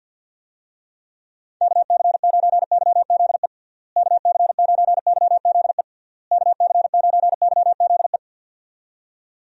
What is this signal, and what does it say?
Morse code, heard as a keyed tone, about 51 wpm, 700 Hz. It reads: KC9YZE KC9YZE KC9YZE
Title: USA Call Signs - 6 Characters - Repeated 3x - 50wpm